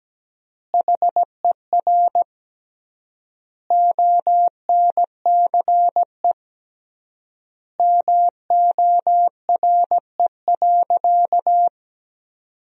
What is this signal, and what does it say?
Morse code, heard as a keyed tone, about 17 wpm, 700 Hz. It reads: HER ONCE MORE.
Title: 3x Word-Spacing - Sentences from Top 500 Words - 17wpm